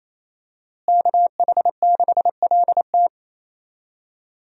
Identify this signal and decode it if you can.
Morse code, heard as a keyed tone, about 28 wpm, 700 Hz. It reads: KH6LT
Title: USA Call Signs - 5 Characters - Repeated 2x - 28wpm